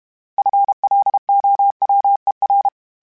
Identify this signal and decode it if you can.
Morse code, heard as a keyed tone, about 32 wpm, 800 Hz. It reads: FLOWER
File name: fromSMs / FromSMs mp3s / Cities@30